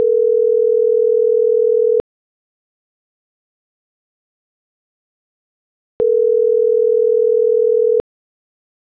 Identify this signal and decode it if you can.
Ringback tone (North America), 440 + 480 Hz.